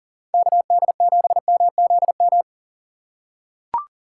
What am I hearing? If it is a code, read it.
Morse code, heard as a keyed tone, about 41 wpm, 700 Hz. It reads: KD7MZM
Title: USA Call Signs - 6 Characters - 4x Character Spacing - 40wpm